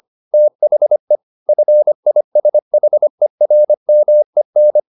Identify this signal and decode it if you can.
Morse code, heard as a keyed tone, about 25 wpm, 600 Hz. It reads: THE FISHERMEN
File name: the old man and the sea part 1 25wpm